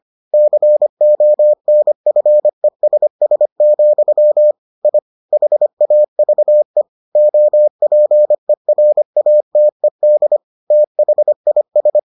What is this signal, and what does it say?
Morse code, heard as a keyed tone, about 25 wpm, 600 Hz. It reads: CONFESS, I HAVE OPERATED THIS